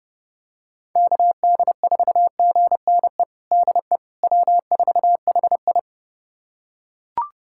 Morse code, heard as a keyed tone, about 32 wpm, 700 Hz. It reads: KD4GNE DE W4HI
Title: USA Callsign DE Callsign 30wpm